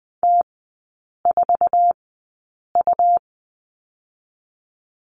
Morse code, heard as a keyed tone, about 20 wpm, spaced as for 8 wpm, 700 Hz. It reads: T4U
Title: 100 Lesson 17e - Letter u - Three Characters 20wpm